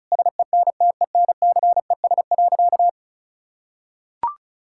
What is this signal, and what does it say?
Morse code, heard as a keyed tone, about 34 wpm, 700 Hz. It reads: SENTENCES.
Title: Pangram Challenge v2 - 35wpm